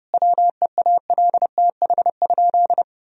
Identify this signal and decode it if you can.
Morse code, heard as a keyed tone, about 30 wpm, 700 Hz. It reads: WEALTH?
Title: Sentences from Top 1000 Words - Review Part 4 - 30wpm